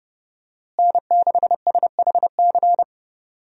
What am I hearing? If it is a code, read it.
Morse code, heard as a keyed tone, about 32 wpm, 700 Hz. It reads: N6SHC
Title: USA Call Signs - 5 Characters - Repeated 3x - 30wpm